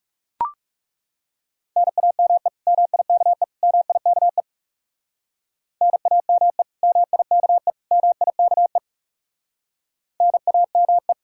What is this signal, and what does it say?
Morse code, heard as a keyed tone, about 37 wpm, 700 Hz. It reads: NAME MIKE MIKE NAME MIKE MIKE NAME
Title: Speed-Racing Common QSO Phrases - 30wpm